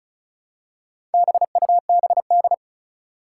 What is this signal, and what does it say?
Morse code, heard as a keyed tone, about 36 wpm, 700 Hz. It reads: BUBD
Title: Four Challenging Letter-Number Combinations 35wpm